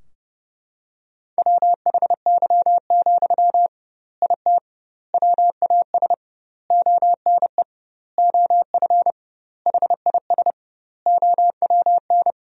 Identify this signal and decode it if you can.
Morse code, heard as a keyed tone, about 30 wpm, 700 Hz. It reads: WHY, IT WAS ONE OF HIS OWN
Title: 2x Word-Spacing - Sentences from Top 300 Words - 30wpm